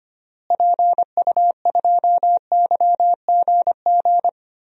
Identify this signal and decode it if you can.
Morse code, heard as a keyed tone, about 25 wpm, 700 Hz. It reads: PU2YGG
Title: DXCC-100 Call Signs - 6 Characters - Repeated 3x - 25wpm